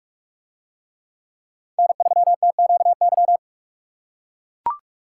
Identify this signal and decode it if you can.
Morse code, heard as a keyed tone, about 44 wpm, 700 Hz. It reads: N3TQY